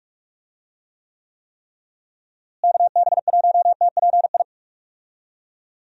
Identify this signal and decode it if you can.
Morse code, heard as a keyed tone, about 46 wpm, 700 Hz. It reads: KB1TPI
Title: USA Call Signs - 6 Characters - Repeated 3x - 45wpm